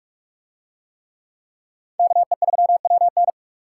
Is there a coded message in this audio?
Yes: Morse code from 2.0 to 3.3 s, heard as a keyed tone, about 44 wpm, 700 Hz. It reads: KE3WN